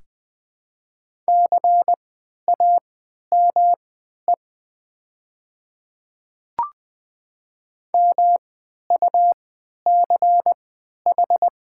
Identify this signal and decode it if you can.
Morse code, heard as a keyed tone, about 20 wpm, spaced as for 11 wpm, 700 Hz. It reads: CAME MUCH
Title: 3x Character Spacing - Top 200 Words - 20wpm